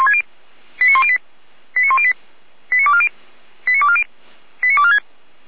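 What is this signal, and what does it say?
Five-tone selective call (ZVEI), heard as quick runs of five tones, tones of about 70 ms. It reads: 89108 89108 89240 89240 89246
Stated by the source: ZVEI five-tone selective call received by SDR, recorded by Feuerwehr Orlamünde (f85360).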